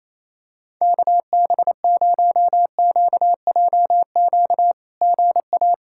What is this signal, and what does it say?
Morse code, heard as a keyed tone, about 28 wpm, 700 Hz. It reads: KB0QJQ GA
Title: SOTA QSO Practice 28wpm